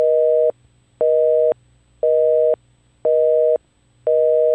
Busy tone (North America), 480 + 620 Hz, with noise.
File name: iLife Sound Effects / Work - Home / Telephone-Busy-Signal